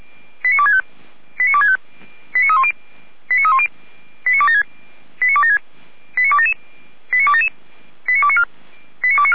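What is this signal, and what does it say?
Five-tone selective call (ZVEI), heard as quick runs of five tones, tones of about 70 ms. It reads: 89265 89265 89310 89276 89276 89277 89277 89294 89294
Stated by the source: ZVEI five-tone selective call received by SDR, from the Feuerwehr Orlamünde audio server.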